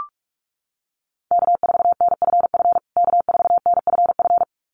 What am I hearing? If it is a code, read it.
Morse code, heard as a keyed tone, about 44 wpm, 700 Hz. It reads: K4NFF K4NFF
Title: USA Call Signs Repeated 2x 45wpm